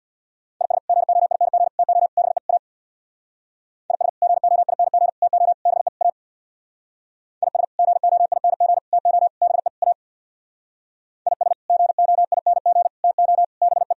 Morse code, heard as a keyed tone, about 64 wpm, 700 Hz. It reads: IS GOING TO BE A IS GOING TO BE A IS GOING TO BE A IS GOING TO BE